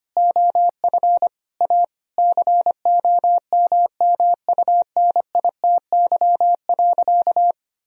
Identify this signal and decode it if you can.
Morse code, heard as a keyed tone, about 25 wpm, 700 Hz. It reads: OF A COMMUNITY.
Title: Sentences from Top 1000 Words - Review Part 4 - 25wpm